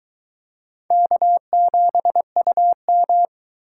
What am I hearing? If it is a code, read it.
Morse code, heard as a keyed tone, about 21 wpm, 700 Hz. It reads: K7UM